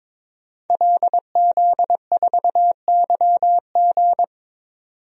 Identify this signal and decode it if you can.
Morse code, heard as a keyed tone, about 22 wpm, 700 Hz. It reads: LZ4YG
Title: DXCC-100 Call Signs - 5 Characters - 22wpm